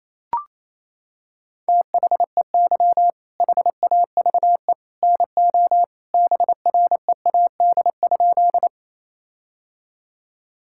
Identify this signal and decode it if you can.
Morse code, heard as a keyed tone, about 28 wpm, 700 Hz. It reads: THEY HAVE NO BREAD?
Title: Sentences from Top 1600 Words 28wpm